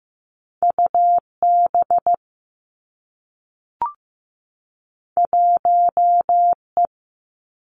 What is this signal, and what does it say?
Morse code, heard as a keyed tone, about 15 wpm, 700 Hz. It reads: UB 1E
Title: Two Challenging Letter-Number Combinations 15wpm